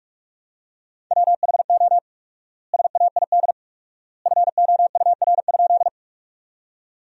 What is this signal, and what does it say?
Morse code, heard as a keyed tone, about 46 wpm, 700 Hz. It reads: WHO SAID FOUR?
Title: 4x Word-Spacing - Sentences from Top 300 Words - 45wpm